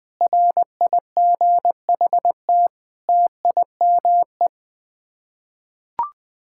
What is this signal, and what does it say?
Morse code, heard as a keyed tone, about 20 wpm, 700 Hz. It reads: RIGHT TIME
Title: Sets of 8 Words - 20wpm